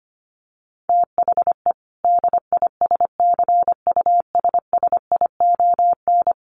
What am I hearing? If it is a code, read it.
Morse code, heard as a keyed tone, about 25 wpm, 700 Hz. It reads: THE DISCUSSION
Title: Sentences from Top 1100 Words 25wpm